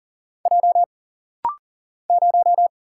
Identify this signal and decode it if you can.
Morse code, heard as a keyed tone, about 40 wpm, 700 Hz. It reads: J 0
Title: Single Letter-Number - Rapid-Fire 40wpm